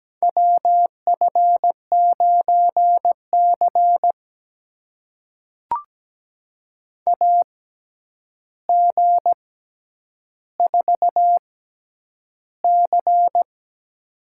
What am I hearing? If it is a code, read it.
Morse code, heard as a keyed tone, about 17 wpm, spaced as for 9 wpm, 700 Hz. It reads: WF9C AG4C
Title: USA Call Signs - 4 Characters - 6x Character Spacing - 17wpm